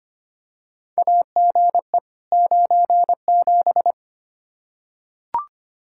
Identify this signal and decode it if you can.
Morse code, heard as a keyed tone, about 25 wpm, 700 Hz. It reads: AGE 97
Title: Practice QSO - Exchange 3 - More Info - 25wpm